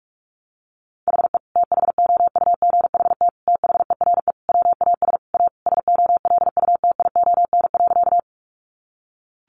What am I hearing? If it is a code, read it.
Morse code, heard as a keyed tone, about 44 wpm, 700 Hz. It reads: HE THOUGHT THERE WAS A SOLUTION.
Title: Sentences from Top 1300 Words 45wpm